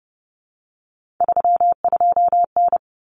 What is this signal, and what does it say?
Morse code, heard as a keyed tone, about 30 wpm, 700 Hz. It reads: 32N
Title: Common Cut Numbers - 3-Digit 30wpm